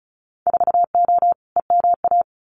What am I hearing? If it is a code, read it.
Morse code, heard as a keyed tone, about 36 wpm, 700 Hz. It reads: 4O EMA
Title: Winter Field Day Exchanges Encore 35wpm